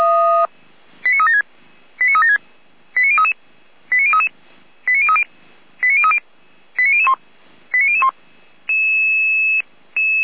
Five-tone selective call (ZVEI), heard as quick runs of five tones, tones of about 70 ms. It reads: 89376 89376 89033 89033 89030 89030 89001 89001